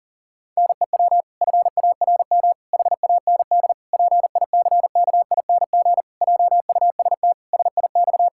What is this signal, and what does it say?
Morse code, heard as a keyed tone, about 40 wpm, 700 Hz. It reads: NEW FARM HAND PICKING JUST SIX
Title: Pangram Challenge v2 - 40wpm